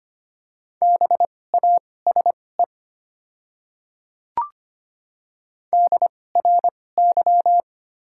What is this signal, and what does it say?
Morse code, heard as a keyed tone, about 25 wpm, spaced as for 18 wpm, 700 Hz. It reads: BASE DRY
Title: 2x Character Spacing - Top 500 Words - 25wpm